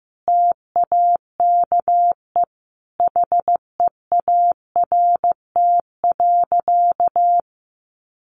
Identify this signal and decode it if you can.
Morse code, heard as a keyed tone, about 15 wpm, 700 Hz. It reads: TAKE HEART.
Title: Sentences from Top 1200 Words 15wpm